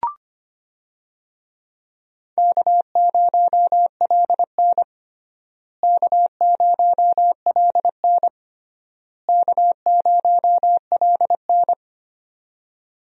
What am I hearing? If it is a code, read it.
Morse code, heard as a keyed tone, about 25 wpm, 700 Hz. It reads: K0LN K0LN K0LN
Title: USA Call Signs - 4 Characters - Repeated 3x - 25wpm